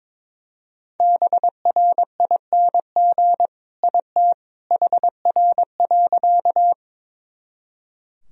Morse code, heard as a keyed tone, about 22 wpm, 700 Hz. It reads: BRING IT HR.
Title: Sentences from Top 500 Words and Abbreviations 22wpm